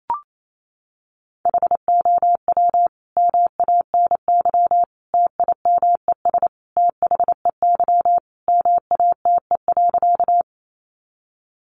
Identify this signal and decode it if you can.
Morse code, heard as a keyed tone, about 28 wpm, 700 Hz. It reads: HOW MANY TIMES THEY MATE.
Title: A Sound Of Thunder 28wpm